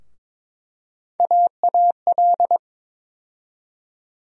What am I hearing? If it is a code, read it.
Morse code, heard as a keyed tone, about 22 wpm, 700 Hz. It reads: AAL
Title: USA Call Sign Postfixes - 3 Letters 22wpm